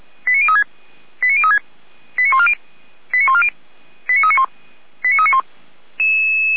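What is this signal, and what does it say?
Five-tone selective call (ZVEI), heard as quick runs of five tones, tones of about 70 ms. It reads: 89036 89036 89140 89140 89391 89391
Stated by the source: ZVEI five-tone selective call 89504 received by SDR.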